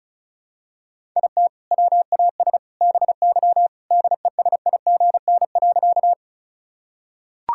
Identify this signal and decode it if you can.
Morse code, heard as a keyed tone, about 36 wpm, 700 Hz. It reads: IT WAS BY DESIGN.
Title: Sentences from Top 1000 Words - Review Part 5 - 35wpm